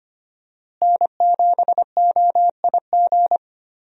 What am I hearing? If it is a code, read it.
Morse code, heard as a keyed tone, about 25 wpm, 700 Hz. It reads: N7OIG
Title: USA Call Signs - 5 Characters - Repeated 3x - 25wpm